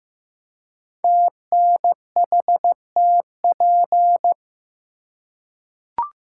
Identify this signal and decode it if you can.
Morse code, heard as a keyed tone, about 15 wpm, 700 Hz. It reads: TNHTP
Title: Code Groups - 5 Letters - 15wpm